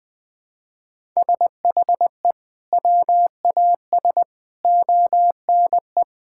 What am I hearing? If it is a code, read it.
Morse code, heard as a keyed tone, about 20 wpm, 700 Hz. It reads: SHE WAS ONE
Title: Sentences with 5-Word Phrases - 20wpm